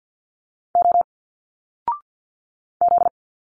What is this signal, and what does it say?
Morse code, heard as a keyed tone, about 50 wpm, 700 Hz. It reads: O Z
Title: Single Letters 50wpm